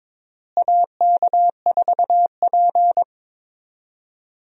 Morse code, heard as a keyed tone, about 22 wpm, 700 Hz. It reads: AK4P
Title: USA Call Signs - 4 Characters - Repeated 2x - 22wpm